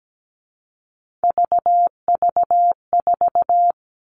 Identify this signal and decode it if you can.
Morse code, heard as a keyed tone, about 17 wpm, 700 Hz. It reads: VV4
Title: All Cut Numbers - 3-Digit 17wpm